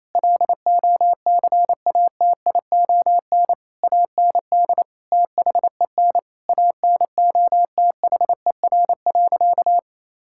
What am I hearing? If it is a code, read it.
Morse code, heard as a keyed tone, about 28 wpm, 700 Hz. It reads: LOCATION AND THEN ANOTHER.
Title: Sentences from Top 1000 Words - Review Part 6 - 28wpm